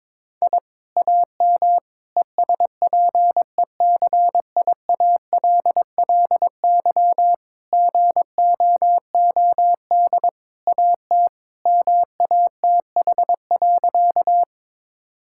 Morse code, heard as a keyed tone, about 22 wpm, 700 Hz. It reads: I AM ESPECIALLY GOOD AT MATH.
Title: Sentences from Top 1000 Words - Review Part 6 - 22wpm